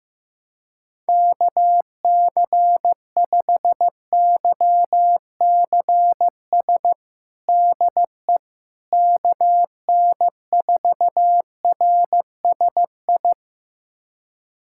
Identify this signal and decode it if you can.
Morse code, heard as a keyed tone, about 15 wpm, 700 Hz. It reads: KC5YCS DE KN4RSI
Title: USA Callsign DE Callsign - 6 Characters - 15wpm